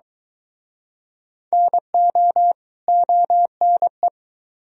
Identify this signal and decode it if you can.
Morse code, heard as a keyed tone, about 23 wpm, 700 Hz. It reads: NO ONE